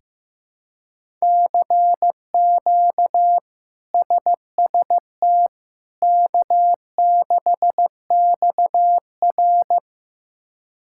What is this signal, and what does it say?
Morse code, heard as a keyed tone, about 15 wpm, 700 Hz. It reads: CQ SST K6XR
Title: K1USN SST QSO Practice - 15wpm